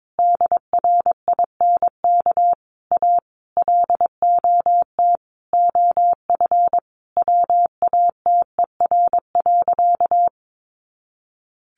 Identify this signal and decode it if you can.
Morse code, heard as a keyed tone, about 22 wpm, 700 Hz. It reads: DRINK A LOT OF WATER.
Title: Additional Easy Phrases 22wpm